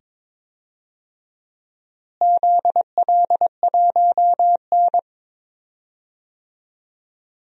Morse code, heard as a keyed tone, about 22 wpm, 700 Hz. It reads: ZL1N
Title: DXCC-100 Call Signs - 4 Characters - Repeated 2x - 22wpm